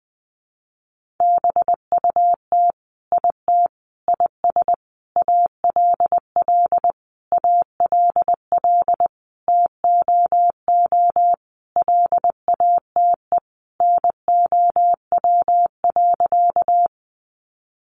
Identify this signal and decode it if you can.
Morse code, heard as a keyed tone, about 20 wpm, 700 Hz. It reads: BUT IT IS ALL ALL TOO LATE NOW.
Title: Sentences from Top 300 Words 20wpm